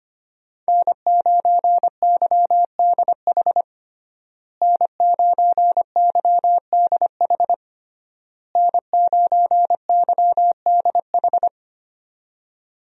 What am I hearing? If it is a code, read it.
Morse code, heard as a keyed tone, about 25 wpm, 700 Hz. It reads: N9YDH N9YDH N9YDH
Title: USA Call Signs - 5 Characters - Repeated 3x - 25wpm